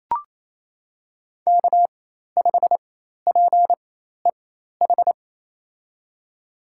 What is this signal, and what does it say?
Morse code, heard as a keyed tone, about 28 wpm, spaced as for 13 wpm, 700 Hz. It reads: K5PEH
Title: USA Call Signs - 5 Characters - 4x Character Spacing - 28wpm